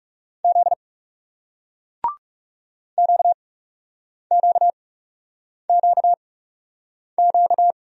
Morse code, heard as a keyed tone, about 40 wpm, spaced as for 8 wpm, 700 Hz. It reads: Z QQQQ